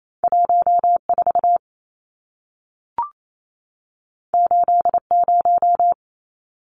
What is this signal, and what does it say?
Morse code, heard as a keyed tone, about 28 wpm, 700 Hz. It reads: 14 80
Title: Numbers - 2-Digit 28wpm